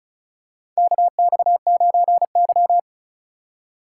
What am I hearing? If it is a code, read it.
Morse code, heard as a keyed tone, about 34 wpm, 700 Hz. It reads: KX9Y
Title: USA Call Signs - 4 Characters - 6x Character Spacing - 35wpm